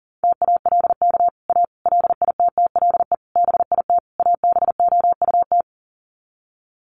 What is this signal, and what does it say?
Morse code, heard as a keyed tone, about 40 wpm, 700 Hz. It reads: TALK A LITTLE BIT ABOUT
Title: Sets of 5 Words Encore 40wpm